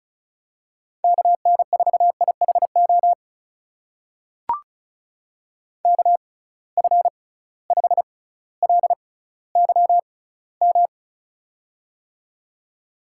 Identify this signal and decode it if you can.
Morse code, heard as a keyed tone, about 36 wpm, spaced as for 18 wpm, 700 Hz. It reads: KN4IHO KF5LYM
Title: USA Call Signs - 6 Characters - 6x Character Spacing - 35wpm